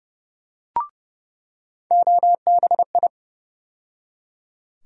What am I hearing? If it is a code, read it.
Morse code, heard as a keyed tone, about 32 wpm, 700 Hz. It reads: OBI
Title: USA Call Sign Postfixes - 3 Letters 30wpm